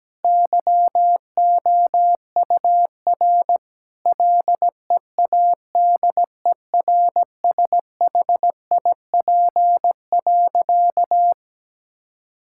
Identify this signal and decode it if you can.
Morse code, heard as a keyed tone, about 17 wpm, 700 Hz. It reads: YOUR LEADERSHIP.
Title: Sentences from Top 1600 Words 17wpm